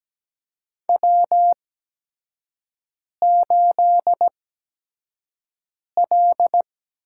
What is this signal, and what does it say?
Morse code, heard as a keyed tone, about 17 wpm, spaced as for 5 wpm, 700 Hz. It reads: W8L
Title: USA Call Signs - 4 Characters - 8x Character Spacing - 17wpm